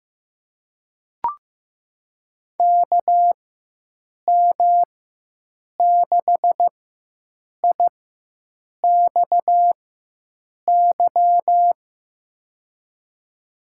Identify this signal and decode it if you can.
Morse code, heard as a keyed tone, about 15 wpm, spaced as for 7 wpm, 700 Hz. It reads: KM6IXY
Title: USA Call Signs - 6 Characters - 4x Character Spacing - 15wpm